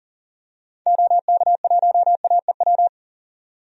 Morse code, heard as a keyed tone, about 40 wpm, 700 Hz. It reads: OK1AEW
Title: DXCC-100 Call Signs - 6 Characters - 40wpm